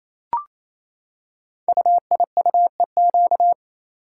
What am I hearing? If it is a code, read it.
Morse code, heard as a keyed tone, about 27 wpm, 700 Hz. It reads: UIUEQ
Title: Code Groups - 5 Letters - 28wpm